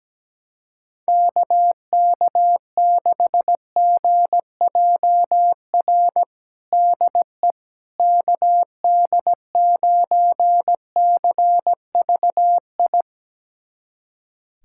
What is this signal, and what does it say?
Morse code, heard as a keyed tone, about 17 wpm, 700 Hz. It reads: KK6GJR DE KD9CVI
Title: USA Callsign DE Callsign - 6 Characters - 17wpm